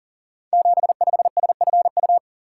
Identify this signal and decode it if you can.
Morse code, heard as a keyed tone, about 42 wpm, 700 Hz. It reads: 75SFU
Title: Code Groups - 5 Letter:Numbers - 40wpm